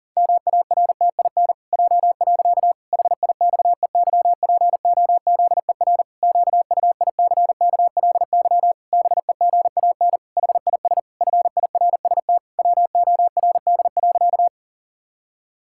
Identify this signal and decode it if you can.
Morse code, heard as a keyed tone, about 40 wpm, 700 Hz. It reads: MARTIN J. HIXEYPOZER QUICKLY BEGAN HIS FIRST WORD.
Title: Pangram Challenge v2 - 40wpm